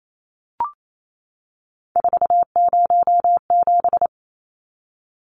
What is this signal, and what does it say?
Morse code, heard as a keyed tone, about 28 wpm, 700 Hz. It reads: 407
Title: Numbers - 3-Digit 28wpm